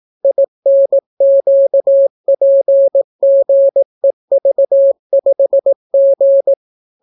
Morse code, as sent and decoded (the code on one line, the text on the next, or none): .. -. --.- .--. --. ...- ..... --.
INQPGEV5G